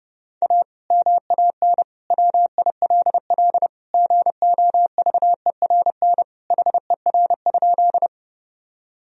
Morse code, heard as a keyed tone, about 30 wpm, 700 Hz. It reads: A MAN WILL GOVERN HER?
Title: Sentences from Top 1000 Words - Review Part 2 - 30wpm